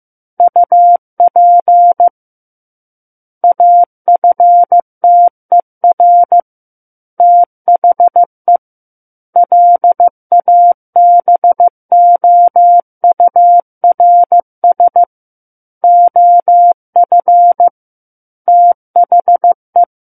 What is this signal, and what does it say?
Morse code, heard as a keyed tone, about 15 wpm, 700 Hz. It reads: UP AFTER THE LABOURS OF THE
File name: Chapter 6 - 15 WPM